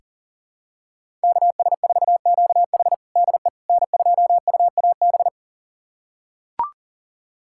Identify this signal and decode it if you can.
Morse code, heard as a keyed tone, about 40 wpm, 700 Hz. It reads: KS4QH DE N2UAB
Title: USA Callsign DE Callsign - 5 Characters - 40wpm